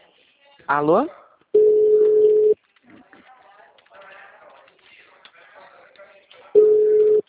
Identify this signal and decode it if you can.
Ringback tone (Europe, 425 Hz), with speech.
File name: out- -2004- - - .42343